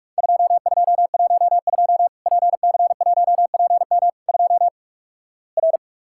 Morse code, heard as a keyed tone, about 48 wpm, 700 Hz. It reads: 2212 PC1PM 2 R
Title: Worked All Europe DX Contest - QTC Exchange - 45wpm